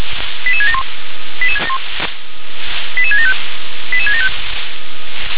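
Five-tone selective call (ZVEI), heard as quick runs of five tones, tones of about 70 ms. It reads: 88671 88675 88675
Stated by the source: ZVEI five-tone call 88620 received by SDR.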